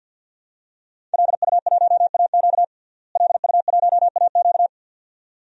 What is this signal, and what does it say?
Morse code, heard as a keyed tone, about 50 wpm, 700 Hz. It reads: LU1AQ LU1AQ
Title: DXCC-100 Call Signs - 5 Characters - Repeated 2x - 50wpm